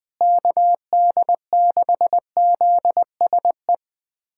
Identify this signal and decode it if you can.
Morse code, heard as a keyed tone, about 20 wpm, 700 Hz. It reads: KD6ZSE
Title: USA Callsign DE Callsign 20wpm